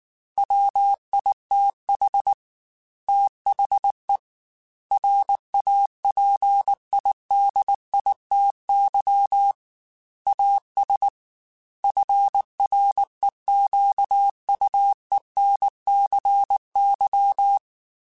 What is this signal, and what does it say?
Morse code, heard as a keyed tone, about 19 wpm, 800 Hz. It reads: WITH THE RAPIDITY AS FREQUENCY
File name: Technical Subjects / Technical Subjects-mp3s / XC @18